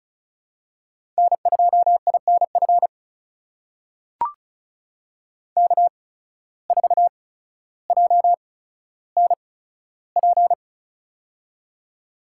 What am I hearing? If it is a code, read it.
Morse code, heard as a keyed tone, about 34 wpm, spaced as for 15 wpm, 700 Hz. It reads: N2INF K4JNP
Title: USA Call Signs - 5 Characters - 8x Character Spacing - 35wpm